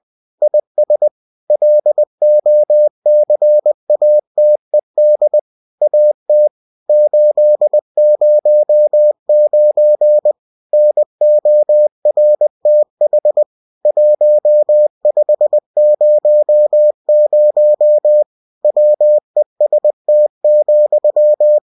Morse code, heard as a keyed tone, about 20 wpm, 600 Hz. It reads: IS LOCATED AT 809 NORTH 1500 WEST,